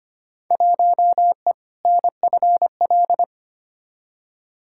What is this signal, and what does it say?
Morse code, heard as a keyed tone, about 25 wpm, 700 Hz. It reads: 1E NFL
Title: Field Day Exchanges - 25wpm